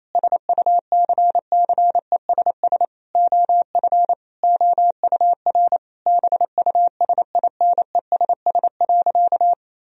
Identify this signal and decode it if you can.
Morse code, heard as a keyed tone, about 28 wpm, 700 Hz. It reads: SUCCESS OF OUR BUSINESS.
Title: Sentences from Top 1000 Words - Review Part 4 - 28wpm